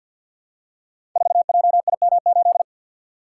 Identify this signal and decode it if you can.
Morse code, heard as a keyed tone, about 50 wpm, 700 Hz. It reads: 4JIM8
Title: Code Groups - 5 Letter:Numbers - 50wpm